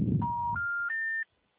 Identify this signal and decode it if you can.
Special information tone, with noise.
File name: out- -2039- - - .55